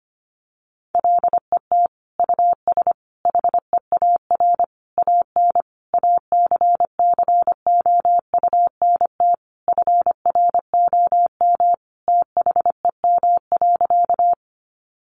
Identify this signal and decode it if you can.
Morse code, heard as a keyed tone, about 25 wpm, 700 Hz. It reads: LET US HEAR AN ACCOUNT FROM THEM.
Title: Sentences from Top 900 Words 25wpm